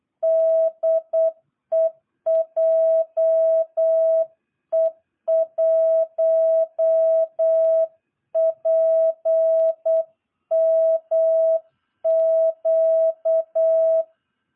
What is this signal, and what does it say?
Morse code, heard as a keyed tone, about 8 wpm, 660 Hz. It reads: DE JE1PMQ
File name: jh8cbh / 2022cw